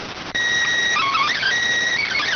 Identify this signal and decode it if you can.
SSTV (slow-scan television), PD 290.